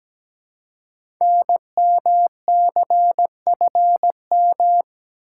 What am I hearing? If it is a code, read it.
Morse code, heard as a keyed tone, about 17 wpm, 700 Hz. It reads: NMCFM
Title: Code Groups - 5 Letters - 17wpm